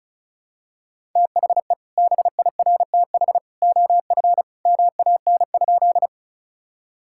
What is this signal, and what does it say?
Morse code, heard as a keyed tone, about 36 wpm, 700 Hz. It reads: THE BIRTH OF MAN?
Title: Sentences from Top 1000 Words - Review Part 5 - 35wpm